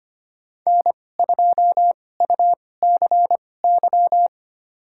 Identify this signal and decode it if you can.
Morse code, heard as a keyed tone, about 25 wpm, spaced as for 18 wpm, 700 Hz. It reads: N2UCY
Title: USA Call Signs - 5 Characters - 2x Character Spacing - 25wpm